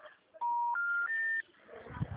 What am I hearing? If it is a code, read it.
Special information tone.